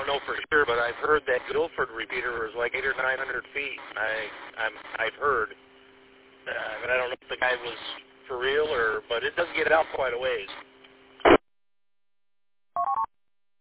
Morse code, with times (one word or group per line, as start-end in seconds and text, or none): none